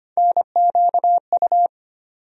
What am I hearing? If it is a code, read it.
Morse code, heard as a keyed tone, about 25 wpm, 700 Hz. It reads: NQU
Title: USA Call Sign Postfixes - 3 Letters 25wpm